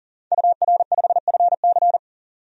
Morse code, heard as a keyed tone, about 37 wpm, 700 Hz. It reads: UR5FC
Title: DXCC-100 Call Signs - 5 Characters - 40wpm